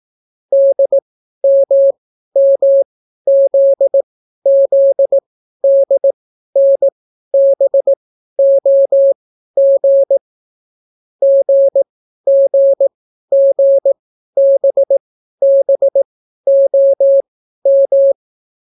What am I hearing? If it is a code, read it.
Morse code, heard as a keyed tone, about 18 wpm, spaced as for 12 wpm, 550 Hz. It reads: DMMZZDNBOG GGGBBOM